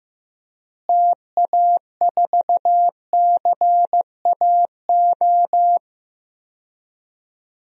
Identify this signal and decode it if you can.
Morse code, heard as a keyed tone, about 15 wpm, 700 Hz. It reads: TA4CAO
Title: DXCC-100 Call Signs - 6 Characters - Repeated 2x - 15wpm